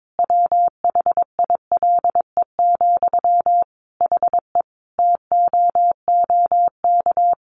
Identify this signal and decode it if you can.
Morse code, heard as a keyed tone, about 22 wpm, 700 Hz. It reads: WHILE, HE TOOK
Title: Sentences from Top 1500 Words 22wpm